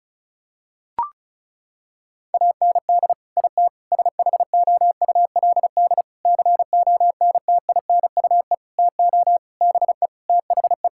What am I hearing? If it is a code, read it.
Morse code, heard as a keyed tone, about 34 wpm, 700 Hz. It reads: AND IT SHOULD CONTINUE TO BE THE
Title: Sentences from Top 1000 Words - Review Part 6 - 35wpm